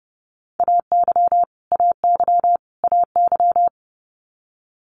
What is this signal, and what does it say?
Morse code, heard as a keyed tone, about 30 wpm, 700 Hz. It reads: AY AY AY
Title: Multi-letter Phonemes Repeated 3x 30wpm